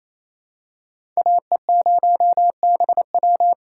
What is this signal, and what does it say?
Morse code, heard as a keyed tone, about 27 wpm, 700 Hz. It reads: AE0BW
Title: USA Call Signs - 2x Character Spacing - 28wpm